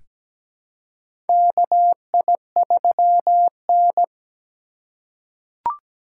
Morse code, heard as a keyed tone, about 17 wpm, 700 Hz. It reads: KI3N
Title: USA Call Signs - 4 Characters - 3x Character Spacing - 17wpm